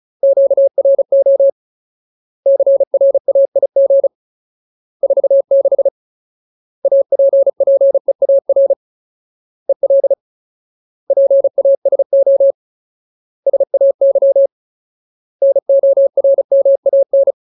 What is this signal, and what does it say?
Morse code, heard as a keyed tone, about 34 wpm, 550 Hz. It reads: QRO CRAIG 46 APPEAR EL PASO SAY NORMAN
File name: Ultimate Word List / 35WPM Ultimate